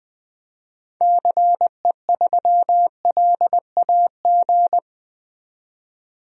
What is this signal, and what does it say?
Morse code, heard as a keyed tone, about 21 wpm, 700 Hz. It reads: CE3LAG
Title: DXCC-100 Call Signs - 6 Characters - 20wpm